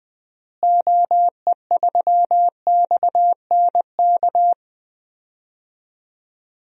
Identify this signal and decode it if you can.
Morse code, heard as a keyed tone, about 21 wpm, 700 Hz. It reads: OE3XNK
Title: DXCC-100 Call Signs - 6 Characters - Repeated 3x - 20wpm